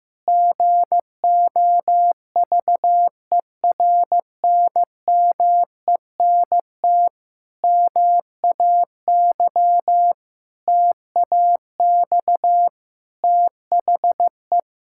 Morse code, heard as a keyed tone, about 16 wpm, 700 Hz. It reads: GOVERNMENT MAY TAX THE
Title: Sentences from Top 1000 Words - Review Part 5 - 15wpm